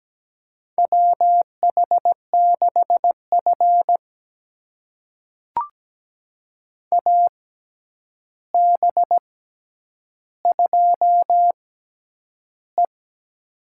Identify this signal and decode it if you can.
Morse code, heard as a keyed tone, about 17 wpm, spaced as for 9 wpm, 700 Hz. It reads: WH6F AB2E
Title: USA Call Signs - 4 Characters - 6x Character Spacing - 17wpm